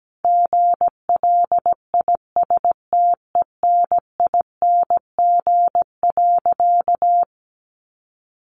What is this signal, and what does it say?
Morse code, heard as a keyed tone, about 17 wpm, 700 Hz. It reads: GLISTENING.
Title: A Sound Of Thunder 17wpm